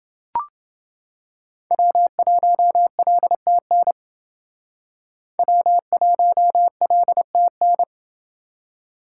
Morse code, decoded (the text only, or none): W1LTN W1LTN